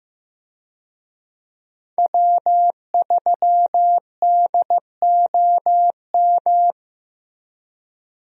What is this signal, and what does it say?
Morse code, heard as a keyed tone, about 15 wpm, 700 Hz. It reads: W3DOM
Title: USA Call Signs - 5 Characters - Repeated 2x - 15wpm